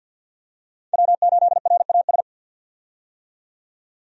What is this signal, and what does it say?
Morse code, heard as a keyed tone, about 48 wpm, 700 Hz. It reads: W8RAS
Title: USA Call Signs - 5 Characters - 50wpm